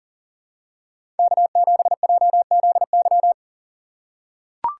Morse code, heard as a keyed tone, about 39 wpm, 700 Hz. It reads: K7JZY
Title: USA Call Signs - 4x Character Spacing - 40wpm